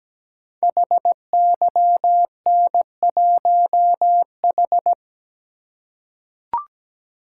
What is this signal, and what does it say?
Morse code, heard as a keyed tone, about 17 wpm, 700 Hz. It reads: HYN1H
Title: Code Groups - 5 Letter:Numbers - 17wpm